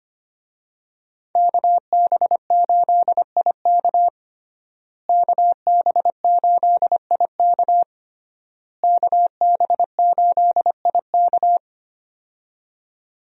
Morse code, heard as a keyed tone, about 25 wpm, 700 Hz. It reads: KB8IK KB8IK KB8IK
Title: USA Call Signs - 5 Characters - Repeated 3x - 25wpm